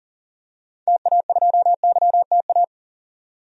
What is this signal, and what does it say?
Morse code, heard as a keyed tone, about 40 wpm, 700 Hz. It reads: TA2YTA
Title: DXCC-100 Call Signs - 6 Characters - 40wpm